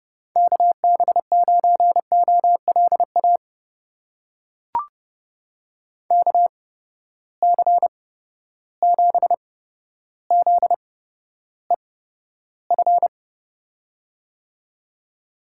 Morse code, heard as a keyed tone, about 30 wpm, spaced as for 13 wpm, 700 Hz. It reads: KB9OLA KC7ZEF
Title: USA Call Signs - 6 Characters - 8x Character Spacing - 30wpm